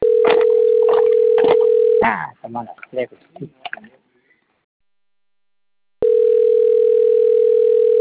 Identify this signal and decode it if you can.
Ringback tone (North America), 440 + 480 Hz, with speech.
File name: rg-1600-unknown- - - .28507